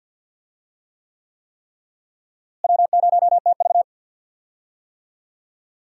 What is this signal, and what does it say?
Morse code, heard as a keyed tone, about 52 wpm, 700 Hz. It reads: W0TV
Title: USA Call Signs - 4 Characters - Repeated 3x - 50wpm